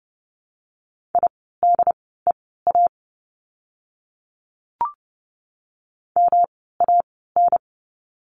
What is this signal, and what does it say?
Morse code, heard as a keyed tone, about 30 wpm, spaced as for 17 wpm, 700 Hz. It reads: IDEA MAN
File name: course / 30:10wpm / 108 Lesson 18f - Letter m - Words 30wpm